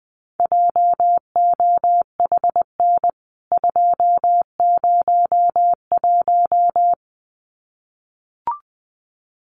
Morse code, heard as a keyed tone, about 20 wpm, 700 Hz. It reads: JOHN 201
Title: CWT Exchange 20wpm